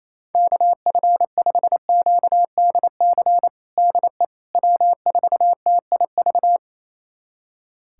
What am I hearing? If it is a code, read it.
Morse code, heard as a keyed tone, about 28 wpm, 700 Hz. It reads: KF5QDC DE W4TIV
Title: USA Callsign DE Callsign 28wpm